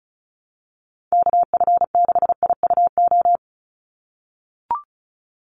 Morse code, heard as a keyed tone, about 34 wpm, 700 Hz. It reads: KF6IUO
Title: USA Call Signs 35wpm